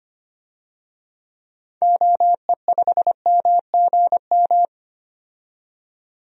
Morse code, heard as a keyed tone, about 25 wpm, 690 Hz. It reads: OE5MGM
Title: DXCC-100 Call Signs - 6 Characters - 25wpm